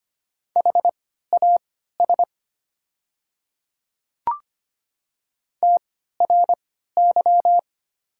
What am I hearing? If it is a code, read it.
Morse code, heard as a keyed tone, about 25 wpm, spaced as for 14 wpm, 700 Hz. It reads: HAS TRY
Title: 3x Character Spacing - Top 300 Words - 25wpm